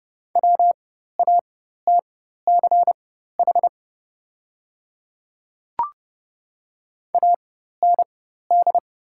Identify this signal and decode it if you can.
Morse code, heard as a keyed tone, about 30 wpm, spaced as for 14 wpm, 700 Hz. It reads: WATCH AND
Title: 4x Character Spacing - Top 300 Words - 30wpm